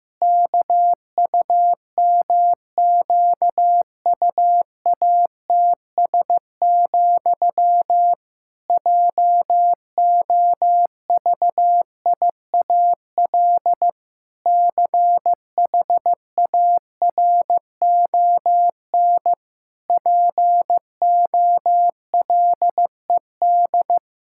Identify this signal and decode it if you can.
Morse code, heard as a keyed tone, about 16 wpm, 700 Hz. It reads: KUMQUATS, JOVIAL CHARON POLED
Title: Pangram Challenge v2 - 15wpm